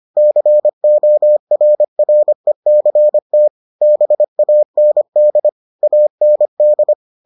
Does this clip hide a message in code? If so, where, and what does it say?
Yes: Morse code from 0.2 to 6.9 s, heard as a keyed tone, about 25 wpm, 600 Hz. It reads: CORRECT BAND AND